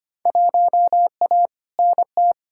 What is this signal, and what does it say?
Morse code, heard as a keyed tone, about 25 wpm, 700 Hz. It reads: 1A NT
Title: Field Day Exchanges Encore - 25wpm